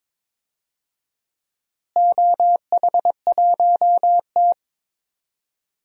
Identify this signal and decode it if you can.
Morse code, heard as a keyed tone, about 22 wpm, 700 Hz. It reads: OH1T
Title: DXCC-100 Call Signs - 4 Characters - Repeated 3x - 22wpm